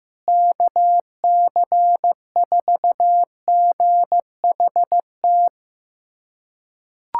Morse code, heard as a keyed tone, about 15 wpm, 700 Hz. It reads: KC4GHT
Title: USA Callsign DE Callsign - 6 Characters - 15wpm